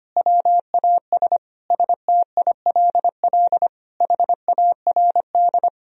Morse code, heard as a keyed tone, about 25 wpm, 700 Hz. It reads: WAS STILL HARD
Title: Sentences from Top 1000 Words - Review Part 3 - 25wpm